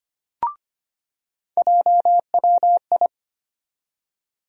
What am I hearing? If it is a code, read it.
Morse code, heard as a keyed tone, about 25 wpm, 700 Hz. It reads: JWI
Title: USA Call Sign Postfixes - 3 Letters 25wpm